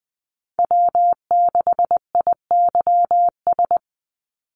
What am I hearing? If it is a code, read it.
Morse code, heard as a keyed tone, about 19 wpm, 700 Hz. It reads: W6IYS
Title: NA SOTA Chaser Callsigns 20wpm